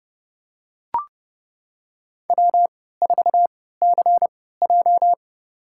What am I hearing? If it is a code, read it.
Morse code, heard as a keyed tone, about 30 wpm, spaced as for 17 wpm, 700 Hz. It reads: W4CJ
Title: USA Call Signs - 3x Character Spacing - 30wpm